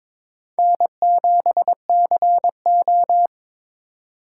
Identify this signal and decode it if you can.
Morse code, heard as a keyed tone, about 22 wpm, 700 Hz. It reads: N7CO
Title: USA Call Signs - 4 Characters - 4x Character Spacing - 22wpm